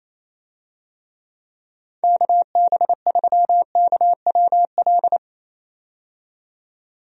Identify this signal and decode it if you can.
Morse code, heard as a keyed tone, about 27 wpm, 700 Hz. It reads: KB3KWL
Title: USA Call Signs - 6 Characters - 28wpm